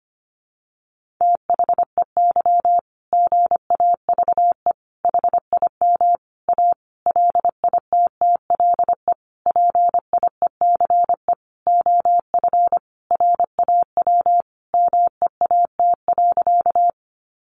Morse code, heard as a keyed tone, about 25 wpm, 700 Hz. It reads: THEY GAVE HIM A LITTLE PIECE OF RAW MEAT.
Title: Sentences from Top 1500 Words 25wpm